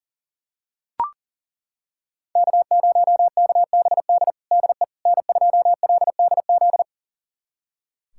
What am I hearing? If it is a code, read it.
Morse code, heard as a keyed tone, about 40 wpm, 700 Hz. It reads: K0KBD DE N2LDZ
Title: USA Callsign DE Callsign - 5 Characters - 40wpm